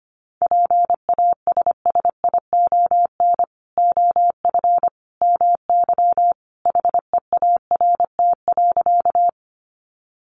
Morse code, heard as a keyed tone, about 25 wpm, 700 Hz. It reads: PASSION OF MY HEART.
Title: Sentences from Top 1300 Words 25wpm